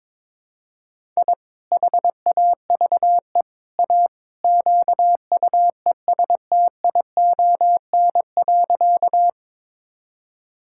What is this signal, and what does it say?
Morse code, heard as a keyed tone, about 22 wpm, 700 Hz. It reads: I HAVE A QUESTION.
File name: practice / 22wpm / Sentences with 3-Word Phrases - 22wpm